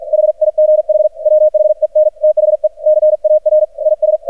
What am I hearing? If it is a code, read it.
Morse code, heard as a keyed tone, about 48 wpm, 610 Hz. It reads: HIGH WHEN THE CAR IS
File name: archived / CRSnet